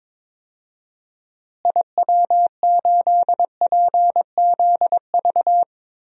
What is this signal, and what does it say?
Morse code, heard as a keyed tone, about 22 wpm, 700 Hz. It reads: IW8PZV
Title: DXCC-100 Call Signs - 6 Characters - Repeated 3x - 22wpm